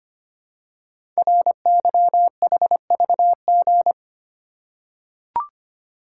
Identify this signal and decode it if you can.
Morse code, heard as a keyed tone, about 25 wpm, 700 Hz. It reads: RYHVG
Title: Code Groups - 5 Letters - 25wpm